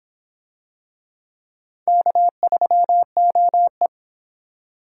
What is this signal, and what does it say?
Morse code, heard as a keyed tone, about 24 wpm, 700 Hz. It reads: K3OE